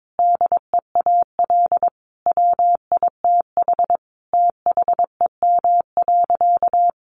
Morse code, heard as a keyed tone, about 22 wpm, 700 Hz. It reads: DEAL WITH THEM.
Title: Sentences from Top 800 Words 22wpm